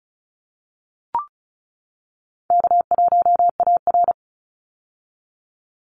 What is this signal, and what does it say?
Morse code, heard as a keyed tone, about 34 wpm, 700 Hz. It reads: K1AR
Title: Call Signs - 35wpm